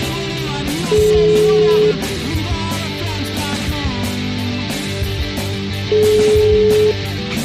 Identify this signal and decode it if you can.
Ringback tone (Europe, 425 Hz), with noise.